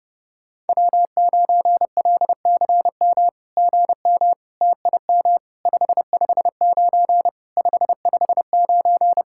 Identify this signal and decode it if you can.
Morse code, heard as a keyed tone, about 28 wpm, 700 Hz. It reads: W9LCM GM TIM 559 559
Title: SOTA QSO Practice 30wpm